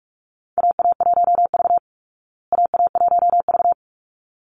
Morse code, heard as a keyed tone, about 44 wpm, 700 Hz. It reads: AA1V AA1V
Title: Call Signs Repeated 3x - 45wpm